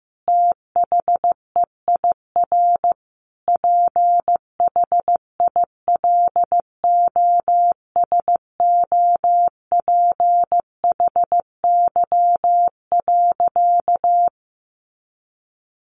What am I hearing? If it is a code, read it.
Morse code, heard as a keyed tone, about 15 wpm, 700 Hz. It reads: THEIR PHILOSOPHY.
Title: Sentences from Top 900 Words 15wpm